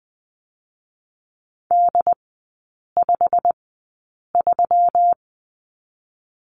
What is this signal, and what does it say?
Morse code, heard as a keyed tone, about 20 wpm, spaced as for 8 wpm, 700 Hz. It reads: D53
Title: 121 Lesson 20e - Number 3 - Three Characters 20wpm